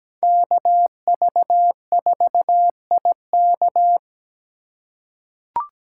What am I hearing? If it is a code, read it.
Morse code, heard as a keyed tone, about 17 wpm, 700 Hz. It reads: KV4IK
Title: USA Call Signs - 5 Characters - 3x Character Spacing - 17wpm